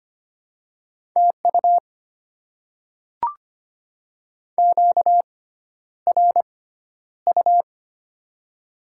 Morse code, heard as a keyed tone, about 25 wpm, spaced as for 11 wpm, 700 Hz. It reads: TU QRU
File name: Top 90 QSO Elements - 6x Character Spacing - 25wpm